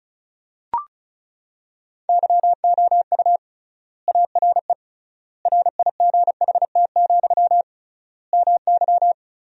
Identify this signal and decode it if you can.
Morse code, heard as a keyed tone, about 34 wpm, 700 Hz. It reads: YOU ARE RIGHT, MY
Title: 3x Word-Spacing - Sentences from Top 400 Words - 35wpm